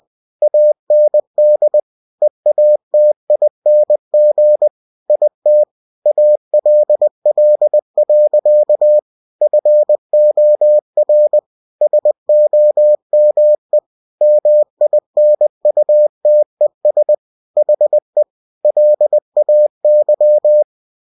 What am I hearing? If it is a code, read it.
Morse code, heard as a keyed tone, about 20 wpm, 600 Hz. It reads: AND EATING IT ALL. FOR SOME MINUTES HE LAY
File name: cw-books / winnie-the-pooh / winnie-20wpm-page-0039